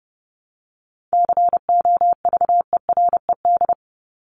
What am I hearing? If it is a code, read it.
Morse code, heard as a keyed tone, about 30 wpm, 700 Hz. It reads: COVERED
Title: ED Suffix - Top 100 Words with Base Words 30wpm